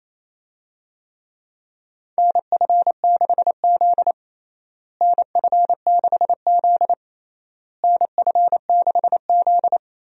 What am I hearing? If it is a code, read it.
Morse code, heard as a keyed tone, about 28 wpm, 700 Hz. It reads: NF6Z NF6Z NF6Z
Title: USA Call Signs - 4 Characters - Repeated 3x - 28wpm